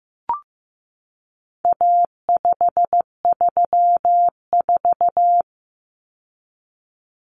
Morse code, heard as a keyed tone, about 15 wpm, 700 Hz. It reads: A534
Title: Common Cut Numbers - 4-Digit 15wpm